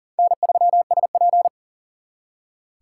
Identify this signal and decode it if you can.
Morse code, heard as a keyed tone, about 50 wpm, 700 Hz. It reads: N3SP
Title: Speed Racing - USA Call Signs 40wpm